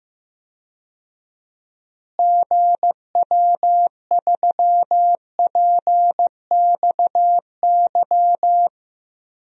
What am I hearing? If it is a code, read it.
Morse code, heard as a keyed tone, about 15 wpm, 700 Hz. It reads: GW3PXY
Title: DXCC-100 Call Signs - 6 Characters - Repeated 3x - 15wpm